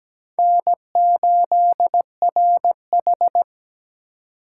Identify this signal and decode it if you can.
Morse code, heard as a keyed tone, about 20 wpm, 700 Hz. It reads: N8RH
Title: Speed Racing - USA Call Signs 17wpm